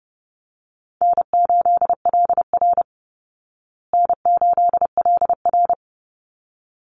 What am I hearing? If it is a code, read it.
Morse code, heard as a keyed tone, about 30 wpm, 700 Hz. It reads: N8LR N8LR
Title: Easy Call Signs Repeated 2x - 30wpm